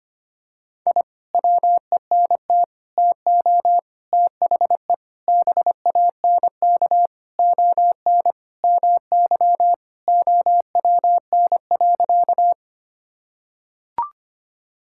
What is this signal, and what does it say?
Morse code, heard as a keyed tone, about 25 wpm, 700 Hz. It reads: I WENT TO THE BANK ON MY OWN.
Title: Sentences from Top 1000 Words - Review Part 6 - 25wpm